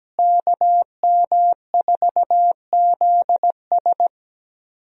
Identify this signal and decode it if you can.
Morse code, heard as a keyed tone, about 17 wpm, 700 Hz. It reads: KM4ZS
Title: USA Call Signs - 5 Characters - Repeated 2x - 17wpm